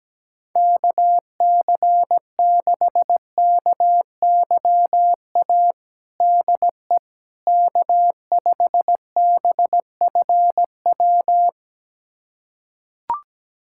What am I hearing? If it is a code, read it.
Morse code, heard as a keyed tone, about 17 wpm, 700 Hz. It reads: KC6KYA DE K5BFW
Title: USA Callsign DE Callsign 17wpm